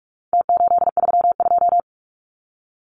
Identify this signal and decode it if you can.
Morse code, heard as a keyed tone, about 44 wpm, 700 Hz. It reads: T832
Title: Numbers - Contest Style - 4-Digit 45wpm